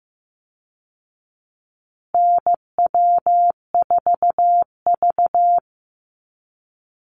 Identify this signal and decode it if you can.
Morse code, heard as a keyed tone, about 15 wpm, 700 Hz. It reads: NW4V
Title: Call Signs Repeated 3x - 15wpm